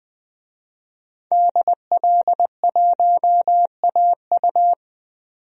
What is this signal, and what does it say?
Morse code, heard as a keyed tone, about 20 wpm, 700 Hz. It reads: DL1AU
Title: DXCC-100 Call Signs - 5 Characters - 20wpm